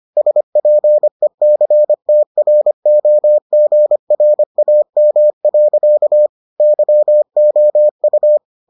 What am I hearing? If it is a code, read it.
Morse code, heard as a keyed tone, about 25 wpm, 600 Hz. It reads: SPECTROGRAM. YOU